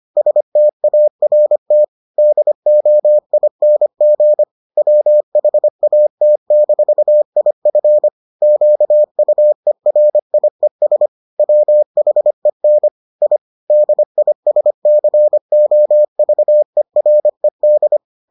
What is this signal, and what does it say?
Morse code, heard as a keyed tone, about 25 wpm, 600 Hz. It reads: START DOING WHAT-IF QUERIES WHEN I DISCOVERED